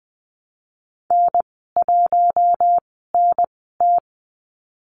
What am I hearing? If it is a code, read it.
Morse code, heard as a keyed tone, about 20 wpm, spaced as for 15 wpm, 700 Hz. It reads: N1NT
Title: 069 Lesson 12g - Letter d - Call Signs 20wpm